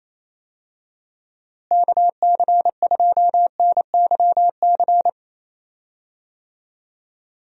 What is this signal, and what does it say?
Morse code, heard as a keyed tone, about 28 wpm, 700 Hz. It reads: KC2NYC
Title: USA Call Signs - 6 Characters - Repeated 3x - 28wpm